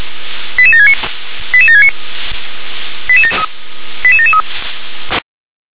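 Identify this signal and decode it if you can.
Five-tone selective call (ZVEI), heard as quick runs of five tones, tones of about 70 ms. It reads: 88769 88769 88773
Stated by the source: ZVEI five-tone selective call 88511, received on SDR.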